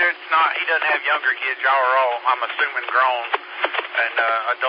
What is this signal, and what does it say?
Emergency Alert System (SAME) tones, heard as harsh buzzing bursts, with speech. It is heard from 1.2 to 4.2 s.